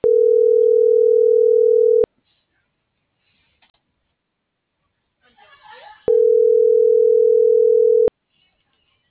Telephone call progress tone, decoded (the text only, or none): ringback tone